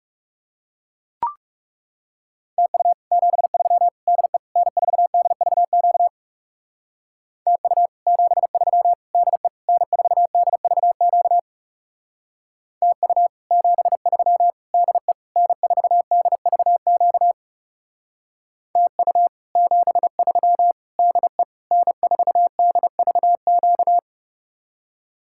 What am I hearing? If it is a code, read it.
Morse code, heard as a keyed tone, about 37 wpm, 700 Hz. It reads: TU 73 DE N4DVQ TU 73 DE N4DVQ TU 73 DE N4DVQ TU 73 DE N4DVQ
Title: Speed-Racing Common QSO Phrases - 30wpm